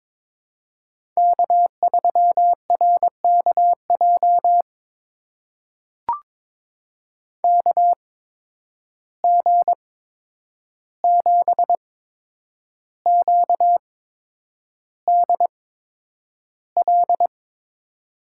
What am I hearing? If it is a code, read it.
Morse code, heard as a keyed tone, about 22 wpm, spaced as for 9 wpm, 700 Hz. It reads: K3RKJ KG7QDL
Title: USA Call Signs - 8x Character Spacing - 22wpm